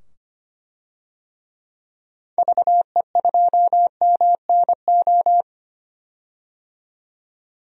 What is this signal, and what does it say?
Morse code, heard as a keyed tone, about 25 wpm, 700 Hz. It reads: VE2MNO
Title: DXCC-100 Call Signs - 6 Characters - 25wpm